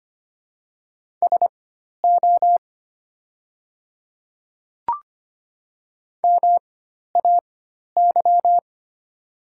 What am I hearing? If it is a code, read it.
Morse code, heard as a keyed tone, about 25 wpm, spaced as for 12 wpm, 700 Hz. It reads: SO MAY
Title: 4x Character Spacing - Top 100 Words - 25wpm